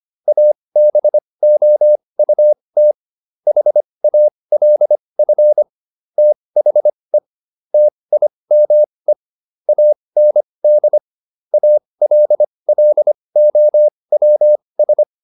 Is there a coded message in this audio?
Yes: Morse code from 0.3 to 15.0 s, heard as a keyed tone, about 25 wpm, spaced as for 20 wpm, 600 Hz. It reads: ABOUT HALF THE TIME AND ALLOWS